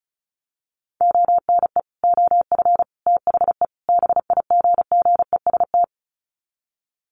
Morse code, heard as a keyed tone, about 34 wpm, 700 Hz. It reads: ONE OF THE BIGGEST
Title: Sets of 4 Words 35wpm